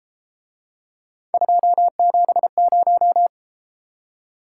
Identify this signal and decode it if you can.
Morse code, heard as a keyed tone, about 33 wpm, 700 Hz. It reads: 270